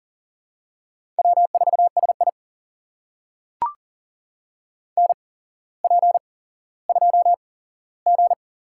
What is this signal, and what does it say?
Morse code, heard as a keyed tone, about 40 wpm, spaced as for 17 wpm, 700 Hz. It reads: W4SI NP2G
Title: USA Call Signs - 4 Characters - 8x Character Spacing - 40wpm